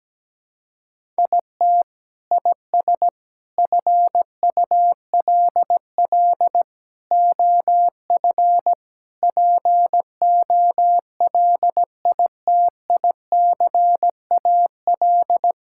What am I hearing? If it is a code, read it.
Morse code, heard as a keyed tone, about 17 wpm, 700 Hz. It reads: IT IS FULL OF POLITICAL
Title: Sentences from Top 1000 Words - Review Part 5 - 17wpm